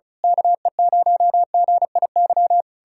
Morse code, heard as a keyed tone, about 34 wpm, 700 Hz. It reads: KE0GIY
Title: Speed-Racing Common QSO Phrases - 30wpm